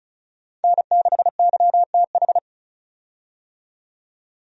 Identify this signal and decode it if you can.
Morse code, heard as a keyed tone, about 34 wpm, 700 Hz. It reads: N6YTH
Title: USA Call Signs - 5 Characters - 35wpm